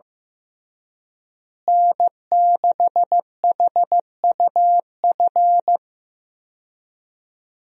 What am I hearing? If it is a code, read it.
Morse code, heard as a keyed tone, about 15 wpm, 700 Hz. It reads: N6HUF
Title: USA Call Signs - 5 Characters - Repeated 2x - 15wpm